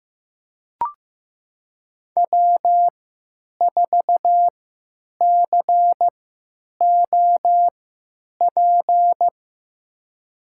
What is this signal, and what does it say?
Morse code, heard as a keyed tone, about 15 wpm, spaced as for 9 wpm, 700 Hz. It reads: W4COP
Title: USA Call Signs - 5 Characters - 3x Character Spacing - 15wpm